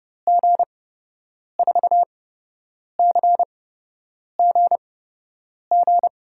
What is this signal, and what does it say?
Morse code, heard as a keyed tone, about 30 wpm, spaced as for 8 wpm, 700 Hz. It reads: G4CGG
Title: USA Call Signs - 8x Character Spacing - 30wpm